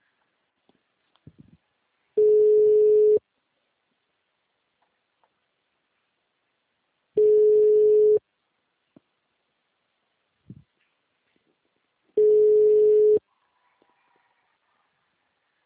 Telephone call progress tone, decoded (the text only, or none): ringback tone